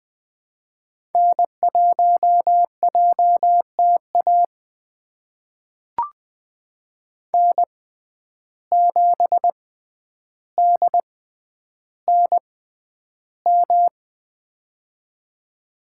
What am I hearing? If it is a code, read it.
Morse code, heard as a keyed tone, about 20 wpm, spaced as for 10 wpm, 700 Hz. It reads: N1JTA N7DNM
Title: USA Call Signs - 5 Characters - 6x Character Spacing - 20wpm